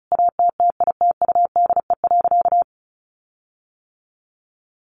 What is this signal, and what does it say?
Morse code, heard as a keyed tone, about 34 wpm, 700 Hz. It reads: ATTITUDE.
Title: Sentences from Top 1300 Words 35wpm